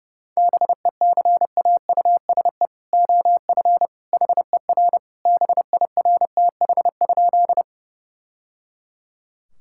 Morse code, heard as a keyed tone, about 30 wpm, 700 Hz. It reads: BECAUSE OF HER BIRTH?
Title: Sentences from Top 1000 Words - Review Part 5 - 30wpm